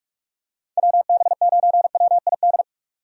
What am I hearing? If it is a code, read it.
Morse code, heard as a keyed tone, about 41 wpm, 700 Hz. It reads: WB9WID
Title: USA Call Signs - 6 Characters - 6x Character Spacing - 45wpm